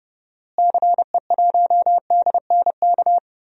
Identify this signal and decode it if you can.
Morse code, heard as a keyed tone, about 31 wpm, 700 Hz. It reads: CE1DNK
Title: DXCC-100 Call Signs - 6 Characters - Repeated 2x - 30wpm